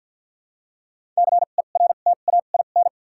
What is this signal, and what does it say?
Morse code, heard as a keyed tone, about 53 wpm, spaced as for 37 wpm, 700 Hz. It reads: CERTAIN
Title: 2x Character Spacing - Top 500 Words - 50wpm